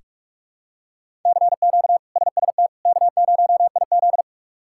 Morse code, heard as a keyed tone, about 46 wpm, 700 Hz. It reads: CQ SST K0IZ
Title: K1USN SST QSO Practice - 45wpm